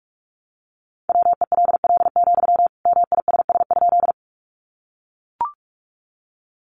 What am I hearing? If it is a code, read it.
Morse code, heard as a keyed tone, about 44 wpm, 700 Hz. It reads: WELL, MISS?
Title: Sentences from Top 500 Words 45wpm